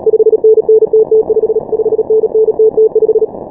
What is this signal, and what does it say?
Morse code, heard as a keyed tone, about 37 wpm, 440 Hz. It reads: 5NN TT5 5NN TT5
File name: final / W1UJ-168-140-MorseRunner